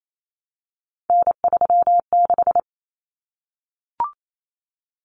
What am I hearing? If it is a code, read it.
Morse code, heard as a keyed tone, about 28 wpm, 700 Hz. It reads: N36
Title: Common Cut Numbers - 3-Digit 28wpm